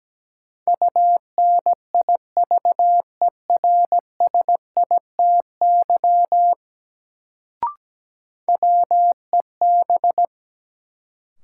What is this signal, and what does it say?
Morse code, heard as a keyed tone, about 17 wpm, 700 Hz. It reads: UNIVERSITY WEB
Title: Sentences from Top 1000 Words - Review Part 4 - New Word Drill - 17wpm